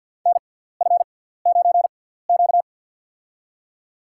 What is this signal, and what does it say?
Morse code, heard as a keyed tone, about 53 wpm, spaced as for 17 wpm, 700 Hz. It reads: NF9Q
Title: USA Call Signs - 4 Characters - 6x Character Spacing - 50wpm